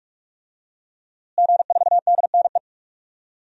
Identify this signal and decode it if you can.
Morse code, heard as a keyed tone, about 46 wpm, 700 Hz. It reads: G4DNE
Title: DXCC-100 Call Signs - 5 Characters - Repeated 3x - 45wpm